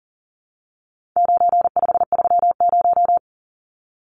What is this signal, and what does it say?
Morse code, heard as a keyed tone, about 40 wpm, 700 Hz. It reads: 9530
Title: Numbers - 4-Digit 40wpm